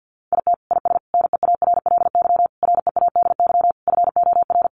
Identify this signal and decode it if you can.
Morse code, heard as a keyed tone, about 50 wpm, 700 Hz. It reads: IT IS NEARLY READY FOR
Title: Sentences from Top 1500 Words 50wpm